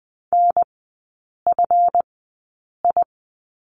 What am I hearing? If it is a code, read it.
Morse code, heard as a keyed tone, about 20 wpm, spaced as for 8 wpm, 700 Hz. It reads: NFI
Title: 191 Lesson 31e - Letter v - Three Characters 20wpm